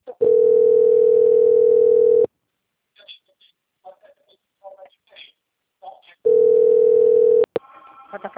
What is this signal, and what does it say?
Ringback tone (North America), 440 + 480 Hz, with speech.
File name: out- - - - - .39047